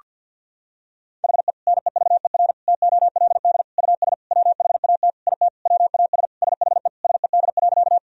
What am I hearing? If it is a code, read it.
Morse code, heard as a keyed tone, about 51 wpm, 700 Hz. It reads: HE NEVER TOLD US WHAT IT WAS SHE SED.
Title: Sentences from Top 400 Words and Abbreviations 50wpm